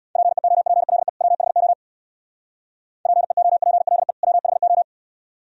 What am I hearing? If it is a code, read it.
Morse code, heard as a keyed tone, about 64 wpm, 690 Hz. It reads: PEOPLE WHO PEOPLE WHO